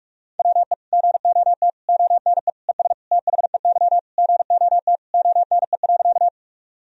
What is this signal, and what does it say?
Morse code, heard as a keyed tone, about 46 wpm, 700 Hz. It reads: WE GOT ONE ES THEY GOT ONE.
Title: Sentences from Top 300 Words and Abbreviations 45wpm